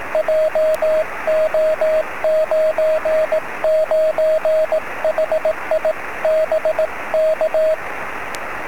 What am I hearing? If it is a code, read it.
Morse code, heard as a keyed tone, about 18 wpm, 620 Hz, with radio noise. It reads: JO99HI BK
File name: sm7lcb / 02-sm0fzh 23cm